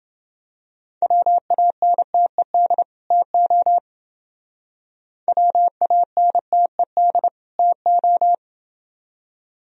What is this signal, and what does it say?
Morse code, heard as a keyed tone, about 29 wpm, 700 Hz. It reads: WANTED TO WANTED TO